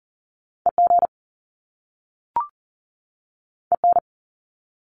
Morse code, heard as a keyed tone, about 41 wpm, 700 Hz. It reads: EG EN
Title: Top Two Letter Combinations in Words 40wpm